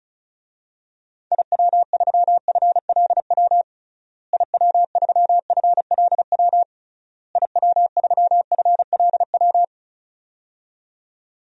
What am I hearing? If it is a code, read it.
Morse code, heard as a keyed tone, about 34 wpm, 700 Hz. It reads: IW3FLW IW3FLW IW3FLW
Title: DXCC-100 Call Signs - 6 Characters - Repeated 3x - 35wpm